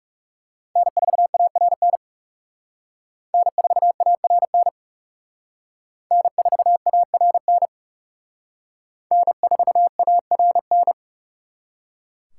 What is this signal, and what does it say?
Morse code, heard as a keyed tone, about 37 wpm, 700 Hz. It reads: N4ARN N4ARN N4ARN N4ARN
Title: Speed Racing - USA 1x3 Call Signs 30wpm